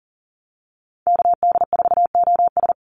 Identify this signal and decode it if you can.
Morse code, heard as a keyed tone, about 40 wpm, 700 Hz. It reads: KD4OS
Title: Call Signs Repeated 3x - 40wpm